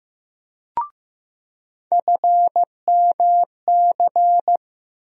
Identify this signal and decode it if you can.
Morse code, heard as a keyed tone, about 15 wpm, 700 Hz. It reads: FMC
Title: USA Call Sign Postfixes - 3 Letters 15wpm